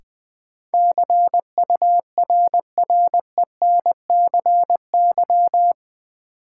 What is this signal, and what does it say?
Morse code, heard as a keyed tone, about 20 wpm, 700 Hz. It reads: CURRENCY
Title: Sentences from Top 1500 Words - New Word Drill - 20wpm